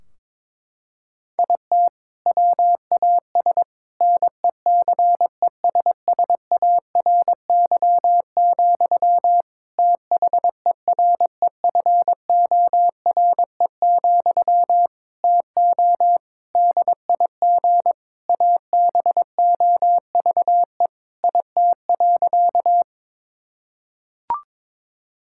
Morse code, heard as a keyed tone, about 22 wpm, 700 Hz. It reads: IT WAS NECESSARY, THEREFORE, TO DIG ABOVE IT.
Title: Sentences from Top 1600 Words 22wpm